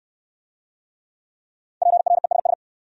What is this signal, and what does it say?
Morse code, heard as a keyed tone, about 65 wpm, 700 Hz. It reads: PRESS